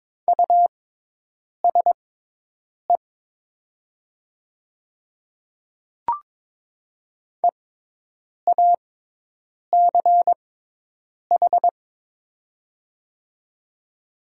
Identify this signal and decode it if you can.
Morse code, heard as a keyed tone, about 22 wpm, spaced as for 8 wpm, 700 Hz. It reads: USE EACH
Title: 6x Character Spacing - Top 200 Words - 22wpm